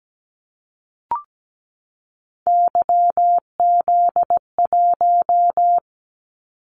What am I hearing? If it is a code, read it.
Morse code, heard as a keyed tone, about 17 wpm, 700 Hz. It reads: YZ1
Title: Callsign Prefixes for Top 25 DXCC - 17wpm